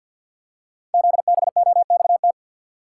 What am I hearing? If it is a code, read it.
Morse code, heard as a keyed tone, about 50 wpm, 700 Hz. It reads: ZBOXT
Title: Code Groups - 5 Letters - 50wpm